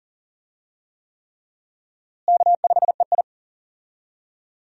Morse code, heard as a keyed tone, about 40 wpm, 700 Hz. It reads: K5EI
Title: USA Call Signs - 4 Characters - Repeated 3x - 40wpm